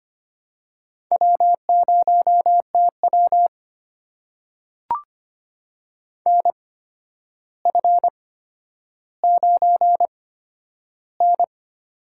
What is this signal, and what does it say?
Morse code, heard as a keyed tone, about 25 wpm, spaced as for 11 wpm, 700 Hz. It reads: W0TW NF9N
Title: USA Call Signs - 4 Characters - 8x Character Spacing - 25wpm